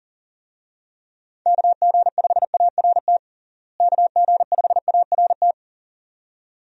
Morse code, heard as a keyed tone, about 40 wpm, 700 Hz. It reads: KG5ART KG5ART
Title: USA Call Signs - 6 Characters - Repeated 2x - 40wpm